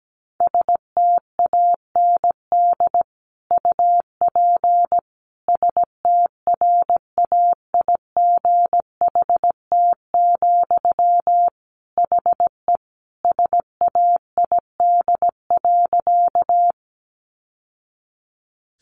Morse code, heard as a keyed tone, about 17 wpm, 700 Hz. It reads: STAND UP STRAIGHT, HE SAID.
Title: Sentences from Top 1000 Words 17wpm